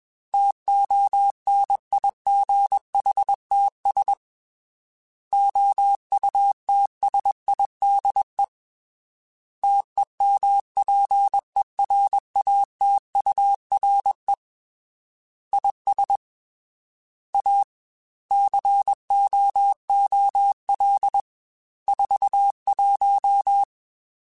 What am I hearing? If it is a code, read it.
Morse code, heard as a keyed tone, about 21 wpm, 800 Hz. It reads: TONIGHTS OUTSIDE TEMPERATURE IS A COOL 41
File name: Weather Sets / Weather Sets-mp3s / Weather 4 14 @20